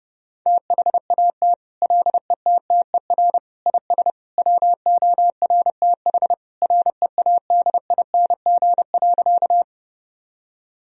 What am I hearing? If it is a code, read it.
Morse code, heard as a keyed tone, about 30 wpm, 700 Hz. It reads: THAT LETTER IS WORTH READING.
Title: Sentences from Top 1000 Words - Review Part 3 - 30wpm